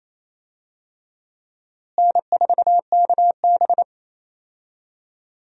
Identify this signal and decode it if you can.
Morse code, heard as a keyed tone, about 27 wpm, 700 Hz. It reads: N4KB
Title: USA Call Signs - 4 Characters - Repeated 2x - 28wpm